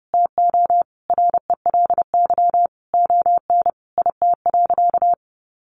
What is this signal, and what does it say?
Morse code, heard as a keyed tone, about 30 wpm, 700 Hz. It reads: TO RELY ON IT.
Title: Sentences from Top 1200 Words 30wpm